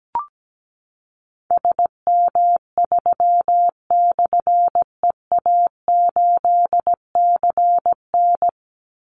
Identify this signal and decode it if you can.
Morse code, heard as a keyed tone, about 17 wpm, 700 Hz. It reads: SM3/EA8CN
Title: Call Signs Repeated 3x - 17wpm